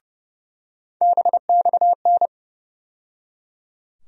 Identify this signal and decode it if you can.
Morse code, heard as a keyed tone, about 31 wpm, 700 Hz. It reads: BXN
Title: USA Call Sign Postfixes - 3 Letters 30wpm